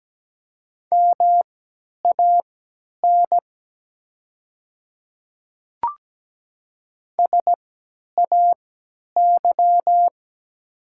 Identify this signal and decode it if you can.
Morse code, heard as a keyed tone, about 17 wpm, spaced as for 10 wpm, 700 Hz. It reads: MAN SAY
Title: 3x Character Spacing - Top 200 Words - 17wpm